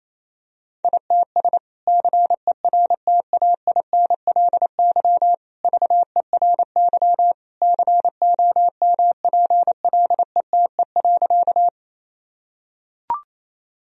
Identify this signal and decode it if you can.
Morse code, heard as a keyed tone, about 27 wpm, 700 Hz. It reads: ITS CERTAINLY VERY COMPLETE.
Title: Sentences from Top 1000 Words - Review Part 2 - 28wpm